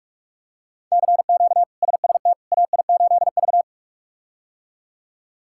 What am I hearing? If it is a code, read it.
Morse code, heard as a keyed tone, about 42 wpm, 700 Hz. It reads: CQ SST AI8V
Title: K1USN SST QSO Practice - 45wpm